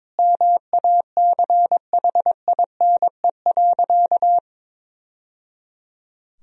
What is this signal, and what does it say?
Morse code, heard as a keyed tone, about 22 wpm, 700 Hz. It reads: MACHINE.
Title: Sentences from Top 1000 Words - Review Part 2 - 22wpm